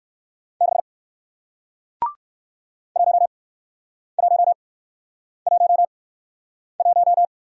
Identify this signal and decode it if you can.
Morse code, heard as a keyed tone, about 58 wpm, spaced as for 9 wpm, 700 Hz. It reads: 6 1111